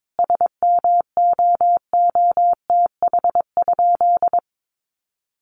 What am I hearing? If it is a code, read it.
Morse code, heard as a keyed tone, about 22 wpm, 700 Hz. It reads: SMOOTH?
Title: Sentences from Top 1400 Words 22wpm